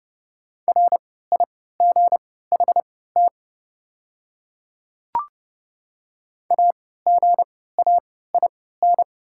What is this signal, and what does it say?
Morse code, heard as a keyed tone, about 30 wpm, spaced as for 17 wpm, 690 Hz. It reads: RIGHT AGAIN
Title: 3x Character Spacing - Top 200 Words - 30wpm